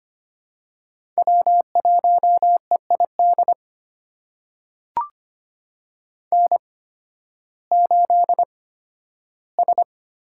Morse code, heard as a keyed tone, about 25 wpm, spaced as for 13 wpm, 700 Hz. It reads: W1EID N8S
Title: USA Call Signs - 8x Character Spacing - 25wpm